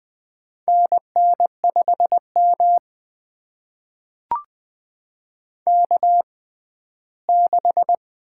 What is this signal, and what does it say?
Morse code, heard as a keyed tone, about 20 wpm, spaced as for 14 wpm, 700 Hz. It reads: NN5M K6
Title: USA Call Signs - 4 Characters - 6x Character Spacing - 20wpm